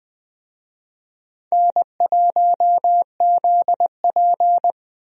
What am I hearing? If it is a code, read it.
Morse code, heard as a keyed tone, about 20 wpm, 700 Hz. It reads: N1ZP
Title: USA Call Signs - 4 Characters - Repeated 3x - 20wpm